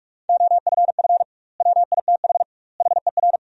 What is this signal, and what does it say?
Morse code, heard as a keyed tone, about 43 wpm, 700 Hz. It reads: OFF WITH HER
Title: 2x Word-Spacing - Sentences from Top 300 Words - 45wpm